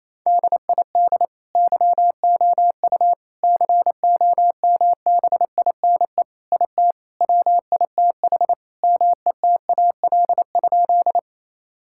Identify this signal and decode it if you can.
Morse code, heard as a keyed tone, about 28 wpm, 700 Hz. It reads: DID YOU COMBINE IT WITH METAL?
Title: Sentences from Top 1600 Words 28wpm